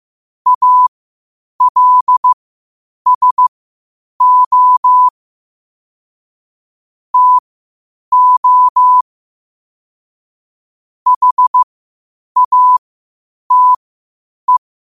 Morse code, heard as a keyed tone, about 15 wpm, spaced as for 8 wpm, 1000 Hz. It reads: ALSO TO HATE
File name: qotdfiles / 8wpm